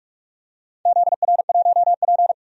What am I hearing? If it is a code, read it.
Morse code, heard as a keyed tone, about 46 wpm, 700 Hz. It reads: ZR1P
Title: DXCC-100 Call Signs - 4 Characters - 45wpm